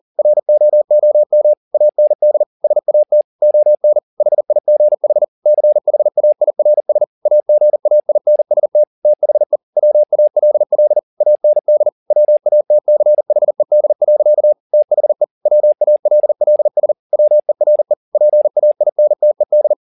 Morse code, heard as a keyed tone, about 40 wpm, 600 Hz. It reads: ROOM AND SAT ON HIGH CHAIRS AGAINST THE WALL AND WATCHED. THE WALLS WERE PAINTED
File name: the old man and the sea 2 40wpm